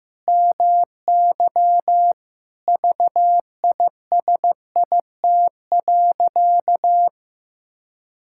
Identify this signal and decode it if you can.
Morse code, heard as a keyed tone, about 16 wpm, 700 Hz. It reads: MY VISIT.
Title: Sentences from Top 1000 Words - Review Part 6 - 15wpm